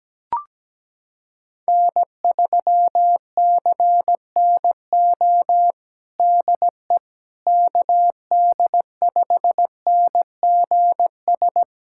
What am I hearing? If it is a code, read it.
Morse code, heard as a keyed tone, about 17 wpm, 700 Hz. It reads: N3CNO DE KD5NGS
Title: USA Callsign DE Callsign 17wpm